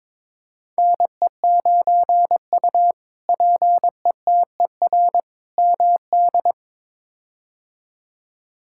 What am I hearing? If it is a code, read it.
Morse code, heard as a keyed tone, about 21 wpm, 700 Hz. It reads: NE9U PETER MD
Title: K1USN SST QSO Practice - 22wpm